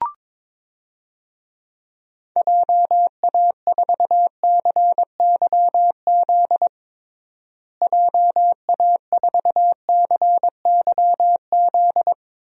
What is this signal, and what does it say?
Morse code, heard as a keyed tone, about 22 wpm, 700 Hz. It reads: JA4CYZ JA4CYZ
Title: DXCC-100 Call Signs - 6 Characters - Repeated 3x - 22wpm